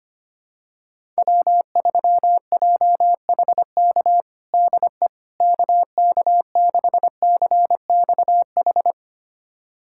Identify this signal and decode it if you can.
Morse code, heard as a keyed tone, about 25 wpm, 700 Hz. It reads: W3JHK DE KK6CXH
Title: USA Callsign DE Callsign 25wpm